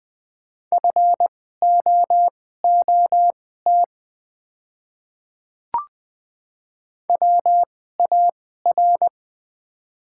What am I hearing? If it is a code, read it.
Morse code, heard as a keyed tone, about 20 wpm, spaced as for 14 wpm, 700 Hz. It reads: FOOT WAR
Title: 2x Character Spacing - Top 500 Words - 20wpm